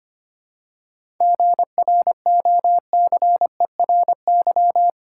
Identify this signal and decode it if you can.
Morse code, heard as a keyed tone, about 25 wpm, 700 Hz. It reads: GROCERY
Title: Seven Letter Words - 25wpm